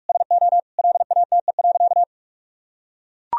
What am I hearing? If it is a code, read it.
Morse code, heard as a keyed tone, about 44 wpm, 700 Hz. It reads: SO LATE.
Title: Sentences from Top 1000 Words - Review Part 3 - 45wpm